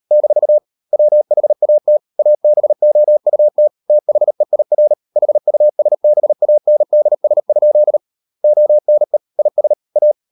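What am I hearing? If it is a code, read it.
Morse code, heard as a keyed tone, about 38 wpm, 600 Hz. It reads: - WHAT ABOUT THEIR HUSBANDS? ONE IS A
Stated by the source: Morse code practice at 38 WPM, sending Ken Follett's novel Never in English, from CW med Gnister.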